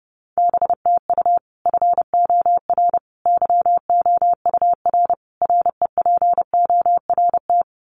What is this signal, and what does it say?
Morse code, heard as a keyed tone, about 30 wpm, 700 Hz. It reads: BTU FOR YOUR REPORT
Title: Common QSO Phrases 30wpm